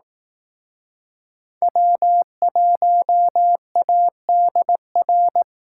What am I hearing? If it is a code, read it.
Morse code, heard as a keyed tone, about 18 wpm, 700 Hz. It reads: W1ADR
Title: Speed Racing - USA 1x3 Call Signs 15wpm